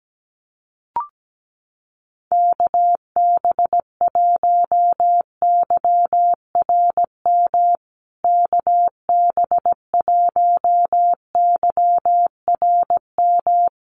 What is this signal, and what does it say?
Morse code, heard as a keyed tone, about 17 wpm, 700 Hz. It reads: KB1YRM KB1YRM
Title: USA Call Signs Repeated 2x 17wpm